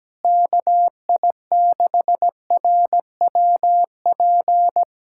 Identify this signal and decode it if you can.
Morse code, heard as a keyed tone, about 17 wpm, 700 Hz. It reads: KI6RWP
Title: USA Callsign DE Callsign - 6 Characters - 17wpm